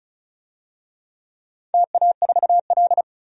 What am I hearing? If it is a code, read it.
Morse code, heard as a keyed tone, about 34 wpm, 700 Hz. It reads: TA4L
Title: DXCC-100 Call Signs - 4 Characters - 35wpm